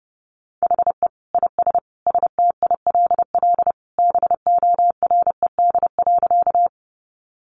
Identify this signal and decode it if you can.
Morse code, heard as a keyed tone, about 30 wpm, 700 Hz. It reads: HE IS STILL BORED.
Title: Easy Phrases 30wpm